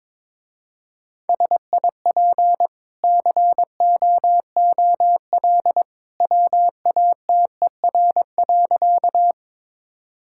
Morse code, heard as a keyed tone, about 22 wpm, 700 Hz. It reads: SIP COOL WATER.
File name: practice / 22wpm / Easy Sentences with Hints - 22wpm